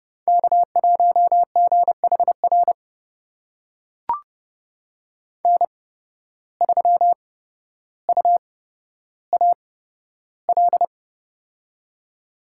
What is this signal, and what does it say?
Morse code, heard as a keyed tone, about 31 wpm, spaced as for 13 wpm, 700 Hz. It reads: K1GHR N3UAL
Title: USA Call Signs - 5 Characters - 8x Character Spacing - 30wpm